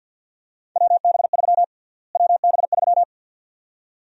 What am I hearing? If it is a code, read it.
Morse code, heard as a keyed tone, about 52 wpm, 700 Hz. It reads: WB3 WB3
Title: USA Call Sign Prefixes - Repeated 2x - 50wpm